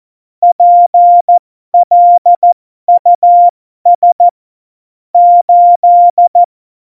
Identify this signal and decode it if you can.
Morse code, heard as a keyed tone, about 14 wpm, spaced as for 12 wpm, 700 Hz. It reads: PLUS 8
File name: Morsekurs / Witze